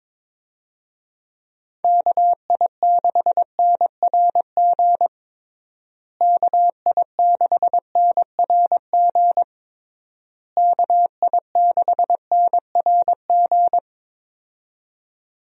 Morse code, heard as a keyed tone, about 22 wpm, 700 Hz. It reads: KI6NRG KI6NRG KI6NRG
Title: USA Call Signs - 6 Characters - Repeated 3x - 22wpm